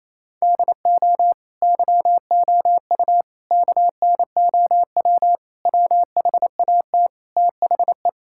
Morse code, heard as a keyed tone, about 28 wpm, 700 Hz. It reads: DO YOU KNOW WHAT THE
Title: Sentences from Top 1600 Words 28wpm